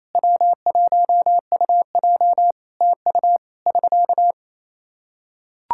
Morse code, heard as a keyed tone, about 28 wpm, 700 Hz. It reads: W1UJ TU <SK>
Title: Practice QSO - Exchange 4 - Ending - 28wpm